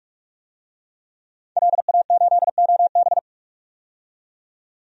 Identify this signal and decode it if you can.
Morse code, heard as a keyed tone, about 45 wpm, 700 Hz. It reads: LA8OB
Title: DXCC-100 Call Signs - 5 Characters - 45wpm